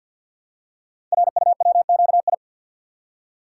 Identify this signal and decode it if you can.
Morse code, heard as a keyed tone, about 48 wpm, 700 Hz. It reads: RUWQI
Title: Code Groups - 5 Letters - 50wpm